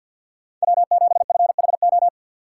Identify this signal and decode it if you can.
Morse code, heard as a keyed tone, about 51 wpm, 700 Hz. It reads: W7FHO
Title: USA Call Signs - 5 Characters - 2x Character Spacing - 50wpm